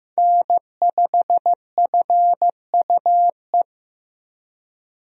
Morse code, heard as a keyed tone, about 15 wpm, 700 Hz. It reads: N5FUE
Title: USA Call Signs - 5 Characters - Repeated 2x - 15wpm